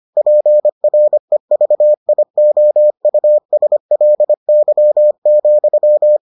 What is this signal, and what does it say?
Morse code, heard as a keyed tone, about 25 wpm, 600 Hz. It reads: PREVIOUSLY,